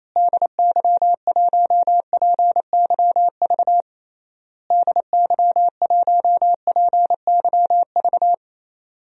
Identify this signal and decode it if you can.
Morse code, heard as a keyed tone, about 28 wpm, 700 Hz. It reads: DY1PYV DY1PYV
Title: DXCC-100 Call Signs - 6 Characters - Repeated 3x - 28wpm